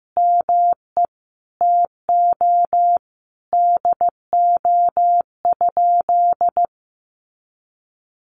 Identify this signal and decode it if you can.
Morse code, heard as a keyed tone, about 15 wpm, 700 Hz. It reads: ME TO DO?
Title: Sentences from Top 200 Words 15wpm